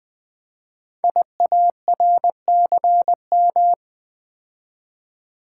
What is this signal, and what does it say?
Morse code, heard as a keyed tone, about 20 wpm, 700 Hz. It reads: IARCM
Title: Code Groups - 5 Letters - 20wpm